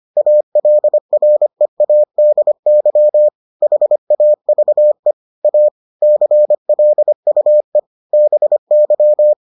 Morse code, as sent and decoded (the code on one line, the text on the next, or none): .- .-.. .-. .- -.. -.-- / .... .- ...- . / .- / -.-. .-.. ..- . / -... -.--
ALREADY HAVE A CLUE BY